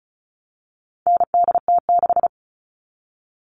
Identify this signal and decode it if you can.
Morse code, heard as a keyed tone, about 34 wpm, 700 Hz. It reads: NDT6
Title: All Cut Numbers - 4-Digit 35wpm